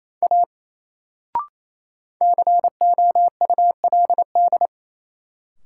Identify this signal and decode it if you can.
Morse code, heard as a keyed tone, about 28 wpm, 700 Hz. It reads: A COULD
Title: Sentences from Top 100 Words - New Word Drill - 28wpm